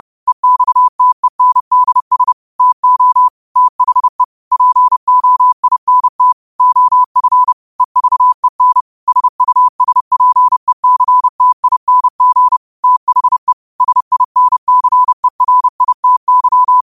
Morse code, heard as a keyed tone, about 30 wpm, 1000 Hz. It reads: EXTENDS TO THE POINT OF EVEN SUSPECTING THE SINCERITY